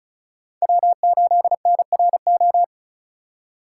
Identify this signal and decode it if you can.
Morse code, heard as a keyed tone, about 46 wpm, 700 Hz. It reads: W8NRO